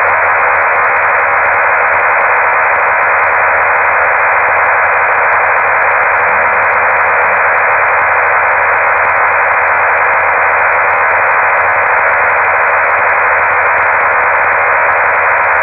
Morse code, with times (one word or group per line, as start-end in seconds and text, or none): none